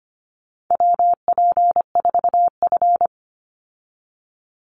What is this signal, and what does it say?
Morse code, heard as a keyed tone, about 25 wpm, 700 Hz. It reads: WP4F
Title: Call Signs - 25wpm